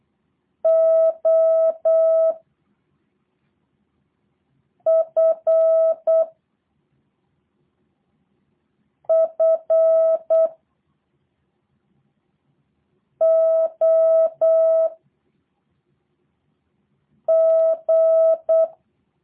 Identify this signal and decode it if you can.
Morse code, heard as a keyed tone, about 8 wpm, 650 Hz. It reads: O F F O G
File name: jh8cbh / 2023cw / review1-4